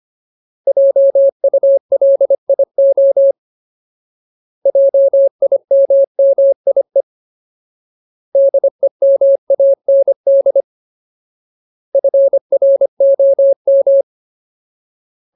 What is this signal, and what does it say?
Morse code, heard as a keyed tone, about 25 wpm, 550 Hz. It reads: JULIO JIMMIE DEMAND FROM
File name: Ultimate Word List / 25WPM Ultimate